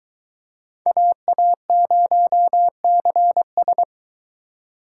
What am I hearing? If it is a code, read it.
Morse code, heard as a keyed tone, about 22 wpm, 700 Hz. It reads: AA0CS